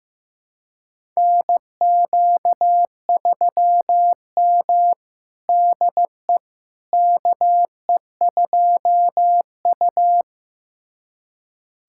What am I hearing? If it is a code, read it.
Morse code, heard as a keyed tone, about 16 wpm, 700 Hz. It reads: NQ3M DE KE2U
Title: USA Callsign DE Callsign - 4 Characters - 15wpm